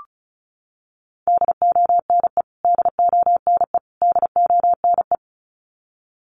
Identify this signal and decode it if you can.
Morse code, heard as a keyed tone, about 34 wpm, 700 Hz. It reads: DONE DONE DONE
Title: Top 500 Words Repeated 35wpm